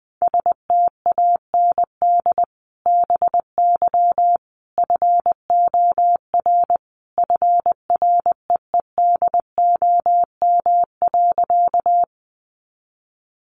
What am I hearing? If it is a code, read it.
Morse code, heard as a keyed tone, about 20 wpm, 700 Hz. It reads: STAND BY FOR FREEDOM.
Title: Sentences from Top 800 Words 20wpm